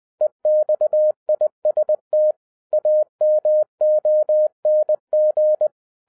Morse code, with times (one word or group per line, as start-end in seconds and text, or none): 0.2-2.3 EXIST
2.7-5.7 AMONG